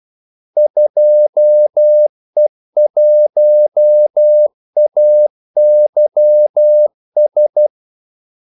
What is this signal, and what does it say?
Morse code, heard as a keyed tone, about 12 wpm, 600 Hz. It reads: 2E1AYS
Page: 2 ekstra ordmellemrum mellem kaldesignalerne